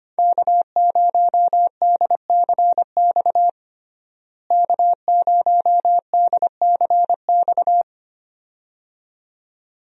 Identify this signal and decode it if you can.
Morse code, heard as a keyed tone, about 25 wpm, 700 Hz. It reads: K0DCX K0DCX
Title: USA Call Signs - 5 Characters - Repeated 2x - 25wpm